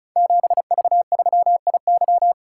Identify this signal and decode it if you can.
Morse code, heard as a keyed tone, about 36 wpm, 700 Hz. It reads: 7V3IY
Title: Code Groups - 5 Letter:Numbers - 35wpm